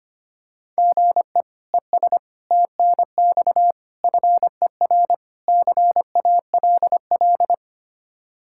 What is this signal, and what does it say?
Morse code, heard as a keyed tone, about 25 wpm, 700 Hz. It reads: GE ES TNX FER CALL
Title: Practice QSO - Exchange 1 - Beginning - 25wpm